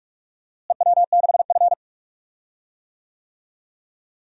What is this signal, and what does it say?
Morse code, heard as a keyed tone, about 47 wpm, 700 Hz. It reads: EW6F
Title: DXCC-100 Call Signs - 4 Characters - Repeated 2x - 45wpm